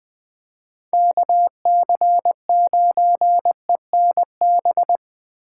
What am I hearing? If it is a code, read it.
Morse code, heard as a keyed tone, about 20 wpm, 700 Hz. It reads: KC9ENB
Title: Speed Racing - USA 2x3 Call Signs 17wpm